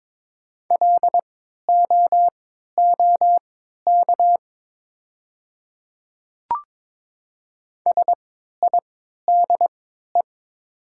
Morse code, heard as a keyed tone, about 22 wpm, spaced as for 13 wpm, 700 Hz. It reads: LOOK SIDE
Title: 3x Character Spacing - Top 200 Words - 22wpm